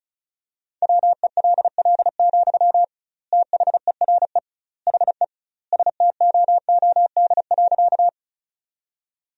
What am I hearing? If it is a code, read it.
Morse code, heard as a keyed tone, about 34 wpm, 700 Hz. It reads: WELL, THERE HE STOOD.
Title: 2x Word-Spacing - Sentences from Top 500 Words - 35wpm